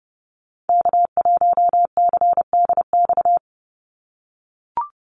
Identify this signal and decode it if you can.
Morse code, heard as a keyed tone, about 30 wpm, 700 Hz. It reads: K1CDX
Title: Call Signs - 30wpm